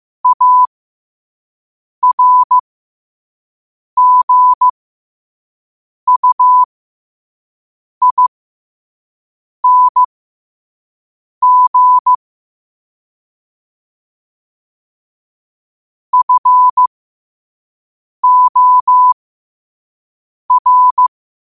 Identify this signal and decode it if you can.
Morse code, heard as a keyed tone, about 15 wpm, spaced as for 5 wpm, 1000 Hz. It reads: ARGUING FOR